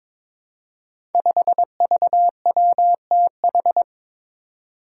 Morse code, heard as a keyed tone, about 22 wpm, 700 Hz. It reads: 5VWTH
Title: Five Challenging Letter-Number Combinations 22wpm